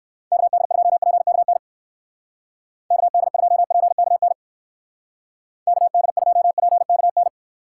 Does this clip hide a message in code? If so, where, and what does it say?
Yes: Morse code from 0.3 to 7.3 s, heard as a keyed tone, about 59 wpm, 700 Hz. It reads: KD2PKN KD2PKN KD2PKN